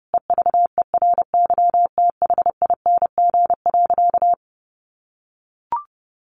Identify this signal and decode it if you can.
Morse code, heard as a keyed tone, about 30 wpm, 700 Hz. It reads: EVERYTHING.
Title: Sentences from Top 700 Words 30wpm